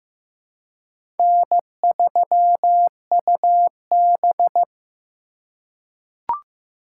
Morse code, heard as a keyed tone, about 15 wpm, 700 Hz. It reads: N3UB
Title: USA Call Signs - 4 Characters - 3x Character Spacing - 15wpm